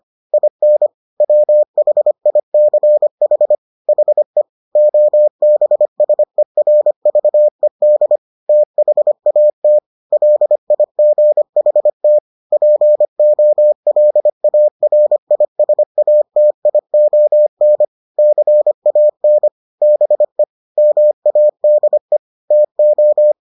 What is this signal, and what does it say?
Morse code, heard as a keyed tone, about 25 wpm, 600 Hz. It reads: IN WHICH HE OBSERVED THAT LIGHT POLARISATION CAN BE MADE TO